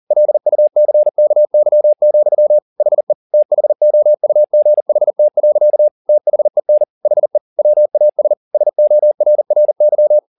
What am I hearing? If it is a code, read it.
Morse code, heard as a keyed tone, about 40 wpm, 600 Hz. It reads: LUCKY, HE THOUGHT. THEN HE WAS SORRY
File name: the old man and the sea 2 40wpm